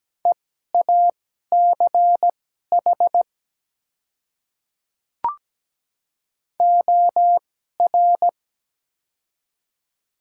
Morse code, heard as a keyed tone, about 17 wpm, spaced as for 12 wpm, 690 Hz. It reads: EACH OR
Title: 2x Character Spacing - Top 100 Words - 17wpm